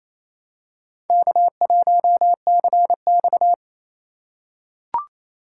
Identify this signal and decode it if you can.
Morse code, heard as a keyed tone, about 28 wpm, 700 Hz. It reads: K1CX
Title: K1USN SST QSO Practice - 28wpm